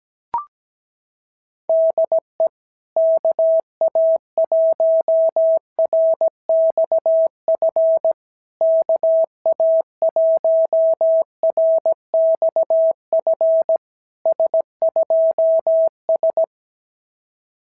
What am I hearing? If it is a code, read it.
Morse code, heard as a keyed tone, about 16 wpm, 650 Hz. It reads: DE KA1RXF KA1RXF S2S
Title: SOTA QSO Practice 17wpm